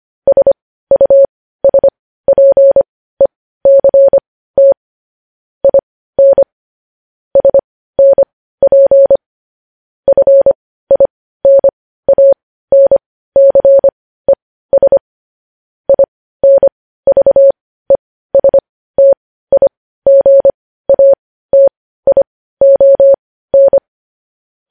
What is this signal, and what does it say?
Morse code, heard as a keyed tone, about 25 wpm, spaced as for 15 wpm, 550 Hz. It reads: SUSPECT IN SNP FINANCES INVESTIGATION
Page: news-headlines-in-morse-code-at-15-wpm